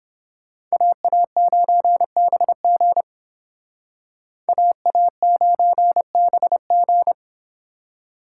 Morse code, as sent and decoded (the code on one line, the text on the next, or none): .- .- ----. -... --. / .- .- ----. -... --.
AA9BG AA9BG